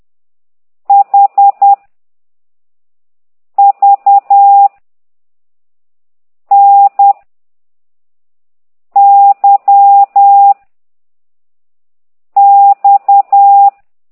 Morse code, heard as a keyed tone, about 10 wpm, 800 Hz. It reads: H V N Y X
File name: ct2idw / exerc9